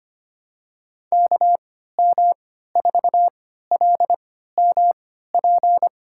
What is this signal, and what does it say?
Morse code, heard as a keyed tone, about 25 wpm, spaced as for 14 wpm, 700 Hz. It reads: KM4LMP
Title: USA Call Signs - 6 Characters - 3x Character Spacing - 25wpm